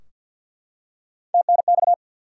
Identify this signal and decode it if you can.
Morse code, heard as a keyed tone, about 52 wpm, 700 Hz. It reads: TNX
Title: Top 90 QSO Elements 50wpm